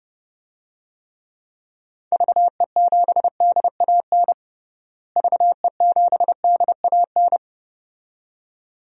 Morse code, heard as a keyed tone, about 29 wpm, 700 Hz. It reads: VE7DAN VE7DAN
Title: DXCC-100 Call Signs - 6 Characters - Repeated 2x - 30wpm